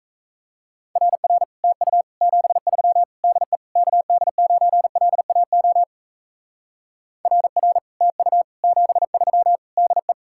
Morse code, heard as a keyed tone, about 34 wpm, 700 Hz. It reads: RR TU 73 DE KD9LAO RR TU 73 DE
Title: Speed-Racing Common QSO Phrases - 28wpm